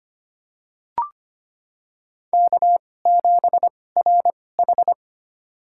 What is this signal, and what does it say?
Morse code, heard as a keyed tone, about 25 wpm, spaced as for 18 wpm, 700 Hz. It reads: K7RH
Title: USA Call Signs - 4 Characters - 2x Character Spacing - 25wpm